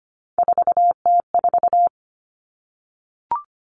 Morse code, heard as a keyed tone, about 25 wpm, 700 Hz. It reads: 4T4
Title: Common Cut Numbers - 3-Digit 25wpm